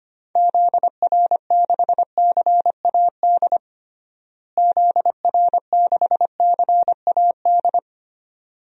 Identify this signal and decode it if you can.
Morse code, heard as a keyed tone, about 25 wpm, 700 Hz. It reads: ZR6CAD ZR6CAD
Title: DXCC-100 Call Signs - 6 Characters - Repeated 2x - 25wpm